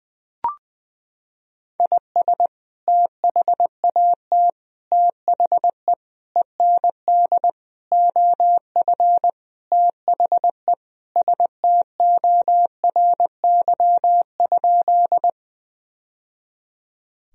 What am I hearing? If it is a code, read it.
Morse code, heard as a keyed tone, about 20 wpm, 700 Hz. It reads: IS THAT THE END OF THE STORY?
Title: Sentences from Top 1000 Words - Review Part 1 - 20wpm